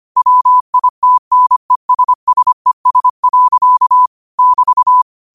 Morse code, heard as a keyed tone, about 25 wpm, 1000 Hz. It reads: WITNESSES. <BT>
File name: qotdfiles / 25wpm